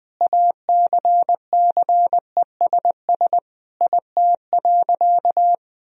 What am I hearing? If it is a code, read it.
Morse code, heard as a keyed tone, about 20 wpm, 700 Hz. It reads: ACCESS IT.
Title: Sentences from Top 1000 Words - Review Part 4 - 20wpm